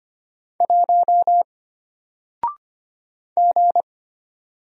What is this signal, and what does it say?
Morse code, heard as a keyed tone, about 25 wpm, 700 Hz. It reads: 1 G
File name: Single-Letter-Number-25wpm